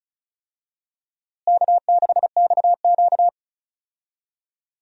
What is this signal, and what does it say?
Morse code, heard as a keyed tone, about 36 wpm, 700 Hz. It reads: K6XQ
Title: USA Call Signs - 4 Characters - Repeated 3x - 35wpm